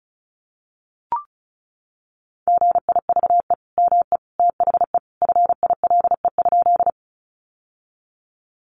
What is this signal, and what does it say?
Morse code, heard as a keyed tone, about 34 wpm, 700 Hz. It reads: GIVE ME THE FILE?
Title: Sentences from Top 1200 Words 35wpm